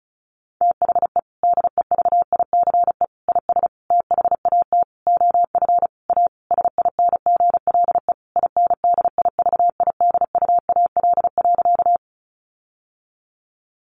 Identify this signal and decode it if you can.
Morse code, heard as a keyed tone, about 34 wpm, 700 Hz. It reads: THE DEVICE IS THAT OF A SINGLE INDIVIDUAL.
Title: Sentences from Top 1500 Words 35wpm